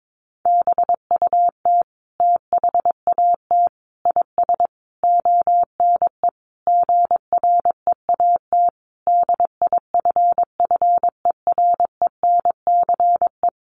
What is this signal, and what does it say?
Morse code, heard as a keyed tone, about 22 wpm, 700 Hz. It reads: BUT THAT IS ONE GREAT DIFFERENCE
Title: Sentences from Top 800 Words 22wpm